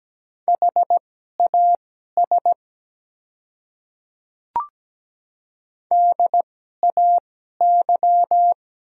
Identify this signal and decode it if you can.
Morse code, heard as a keyed tone, about 17 wpm, spaced as for 12 wpm, 700 Hz. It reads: HAS DAY
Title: 2x Character Spacing - Top 100 Words - 17wpm